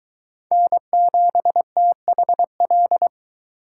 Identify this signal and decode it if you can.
Morse code, heard as a keyed tone, about 22 wpm, 700 Hz. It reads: N7THL